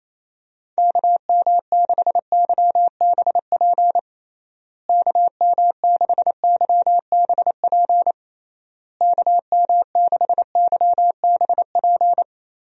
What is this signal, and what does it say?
Morse code, heard as a keyed tone, about 27 wpm, 700 Hz. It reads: KM6YBP KM6YBP KM6YBP
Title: USA Call Signs - 6 Characters - Repeated 3x - 28wpm